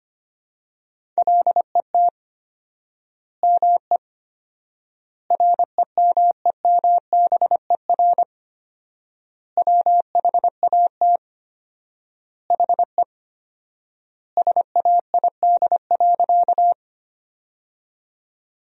Morse code, heard as a keyed tone, about 25 wpm, 700 Hz. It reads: LET ME REMEMBER WHAT HE SAID.
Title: 4x Word-Spacing - Sentences from Top 400 Words - 25wpm